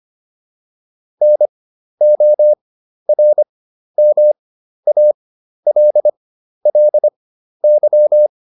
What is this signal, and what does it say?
Morse code, heard as a keyed tone, about 25 wpm, spaced as for 12 wpm, 600 Hz. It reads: NORMALLY